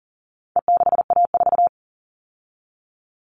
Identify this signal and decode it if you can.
Morse code, heard as a keyed tone, about 40 wpm, 700 Hz. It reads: E6A4
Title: All Cut Numbers - 4-Digit 40wpm